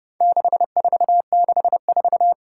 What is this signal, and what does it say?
Morse code, heard as a keyed tone, about 30 wpm, 700 Hz. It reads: 6464
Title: Four Challenging Letter-Number Combinations 30wpm